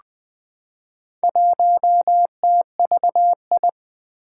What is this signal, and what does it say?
Morse code, heard as a keyed tone, about 20 wpm, 700 Hz. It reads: 1TVI
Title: Four Challenging Letter-Number Combinations 20wpm